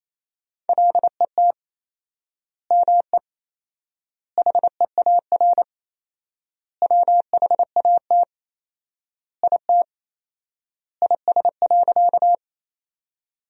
Morse code, heard as a keyed tone, about 28 wpm, 700 Hz. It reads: LET ME HEAR WHAT IT IS.
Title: 4x Word-Spacing - Sentences from Top 300 Words - 28wpm